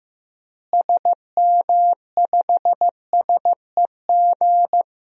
Morse code, heard as a keyed tone, about 15 wpm, 700 Hz. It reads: SM5SEG
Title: DXCC-100 Call Signs - 6 Characters - Repeated 3x - 15wpm